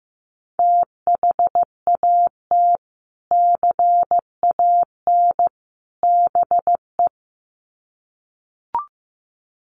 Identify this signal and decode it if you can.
Morse code, heard as a keyed tone, about 15 wpm, 700 Hz. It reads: THAT CAN BE
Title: Sets of 3 Words 15wpm